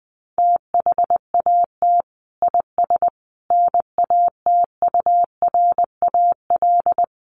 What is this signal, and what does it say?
Morse code, heard as a keyed tone, about 20 wpm, 700 Hz. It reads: THAT IS NATURAL
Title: Sentences from Top 700 Words 20wpm